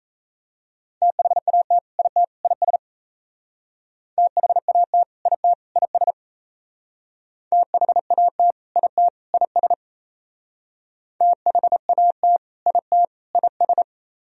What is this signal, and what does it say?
Morse code, heard as a keyed tone, about 36 wpm, 700 Hz. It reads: THAT IT IS THAT IT IS THAT IT IS THAT IT IS